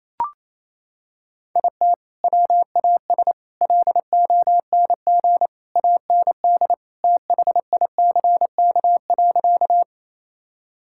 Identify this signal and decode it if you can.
Morse code, heard as a keyed tone, about 28 wpm, 700 Hz. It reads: IT WAS LONG AND THICK.
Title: Sentences from Top 1600 Words 28wpm